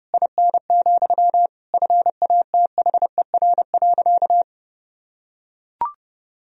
Morse code, heard as a keyed tone, about 32 wpm, 700 Hz. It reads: IN, FATHER.
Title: Sentences from Top 1000 Words - Review Part 2 - 30wpm